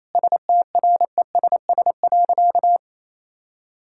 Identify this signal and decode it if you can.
Morse code, heard as a keyed tone, about 28 wpm, 700 Hz. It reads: STRESS.
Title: Sentences from Top 1000 Words - Review Part 6 - 28wpm